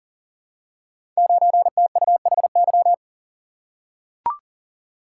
Morse code, heard as a keyed tone, about 38 wpm, 700 Hz. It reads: 9TUHY
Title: Code Groups - 5 Letter:Numbers - 40wpm